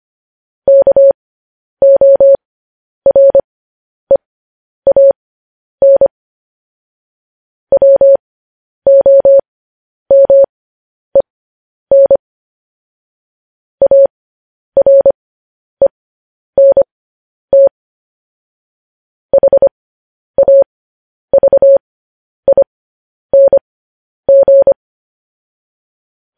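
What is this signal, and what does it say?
Morse code, heard as a keyed tone, about 25 wpm, spaced as for 10 wpm, 550 Hz. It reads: KOREAN WOMEN ARENT HAVING
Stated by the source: Morse code news headlines sent as a reading transcription, preceded by vvv.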